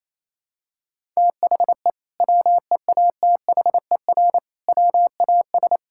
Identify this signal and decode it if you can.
Morse code, heard as a keyed tone, about 28 wpm, 700 Hz. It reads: THE WEATHER WAS
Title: Sentences from Top 1000 Words - Review Part 5 - 28wpm